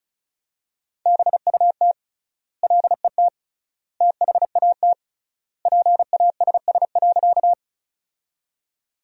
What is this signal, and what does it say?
Morse code, heard as a keyed tone, about 36 wpm, 700 Hz. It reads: BUT LET THAT PASS.
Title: 3x Word-Spacing - Sentences from Top 400 Words - 35wpm